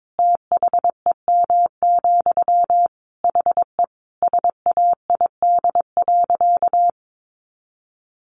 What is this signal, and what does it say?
Morse code, heard as a keyed tone, about 22 wpm, 700 Hz. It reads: THEM, HE SAID.
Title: Sentences from Top 400 Words 22wpm